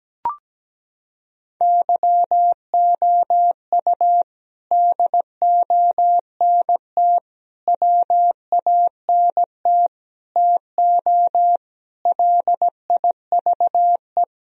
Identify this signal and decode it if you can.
Morse code, heard as a keyed tone, about 17 wpm, 700 Hz. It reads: YOU DONT WANT TO LIVE
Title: Sentences from Top 1000 Words - Review Part 1 - 17wpm